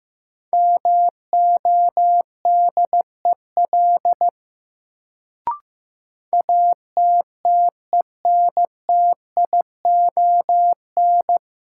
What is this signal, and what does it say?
Morse code, heard as a keyed tone, about 15 wpm, 700 Hz. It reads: MODEL ATTENTION
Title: Sentences from Top 1000 Words - New Word Drill - 15wpm